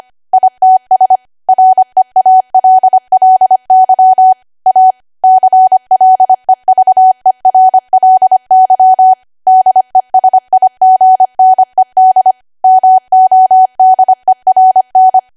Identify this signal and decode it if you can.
Morse code, heard as a keyed tone, about 25 wpm, 750 Hz. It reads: ITS REALLY A CLEVERLY DESIGNED MODERN